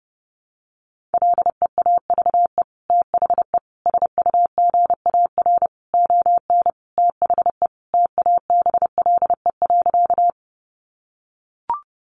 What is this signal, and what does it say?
Morse code, heard as a keyed tone, about 30 wpm, 700 Hz. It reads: LEAVE THE SUGAR ON THE TABLE.
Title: Sentences from Top 1200 Words 30wpm